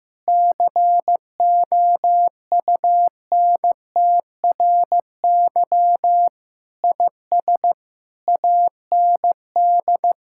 Morse code, heard as a keyed tone, about 16 wpm, 700 Hz. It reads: COUNTRY IS AND
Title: Sentences from Top 1000 Words - Review Part 6 - 15wpm